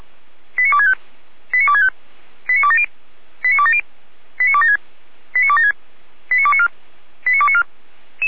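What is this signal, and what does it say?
Five-tone selective call (ZVEI), heard as quick runs of five tones, tones of about 70 ms. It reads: 89265 89265 89270 89270 89276 89276 89294 89294
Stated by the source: ZVEI five-tone call received by SDR, from feuerwehr-orlamuende.org (fire brigade).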